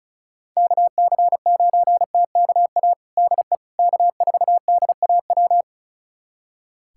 Morse code, heard as a keyed tone, about 34 wpm, 700 Hz. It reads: KC9TKA DE K4DAW
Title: USA Callsign DE Callsign 35wpm